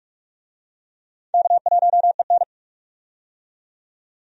Morse code, heard as a keyed tone, about 44 wpm, 700 Hz. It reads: K1EN
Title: USA Call Signs - 4 Characters - 45wpm